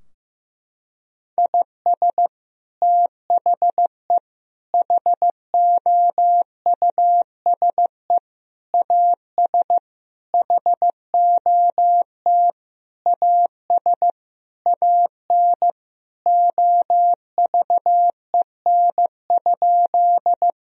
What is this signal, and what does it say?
Morse code, heard as a keyed tone, about 16 wpm, 700 Hz. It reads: IS THE HOUSE AS HOT AS AN OVEN?
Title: Sentences from Top 1000 Words - Review Part 4 - 15wpm